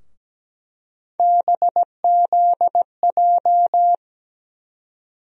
Morse code, heard as a keyed tone, about 17 wpm, 700 Hz. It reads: BZJ
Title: USA Call Sign Postfixes - 3 Letters 17wpm